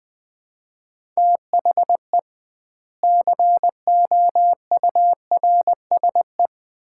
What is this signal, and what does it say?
Morse code, heard as a keyed tone, about 20 wpm, 700 Hz. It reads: THE COURSE
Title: 2x Word-Spacing - Sets of 4 Words - 20wpm